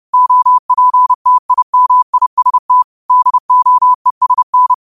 Morse code, heard as a keyed tone, about 30 wpm, 1000 Hz. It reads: OPTIMIST DOESN
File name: qotdfiles / 30wpm